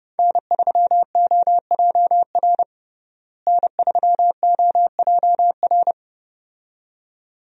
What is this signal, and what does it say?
Morse code, heard as a keyed tone, about 32 wpm, 700 Hz. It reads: N3OJR N3OJR
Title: USA Call Signs - 5 Characters - Repeated 2x - 30wpm